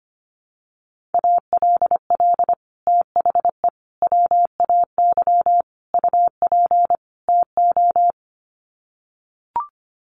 Morse code, heard as a keyed tone, about 25 wpm, 700 Hz. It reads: ALL THE WAY UP TO
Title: Sets of 5 Words Encore 25wpm